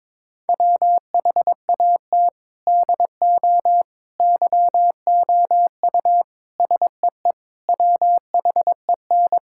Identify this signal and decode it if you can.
Morse code, heard as a keyed tone, about 22 wpm, 700 Hz. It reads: WHAT DO YOU SEE WHEN
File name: practice / 22wpm / Sentences with 4-Word Phrases - 22wpm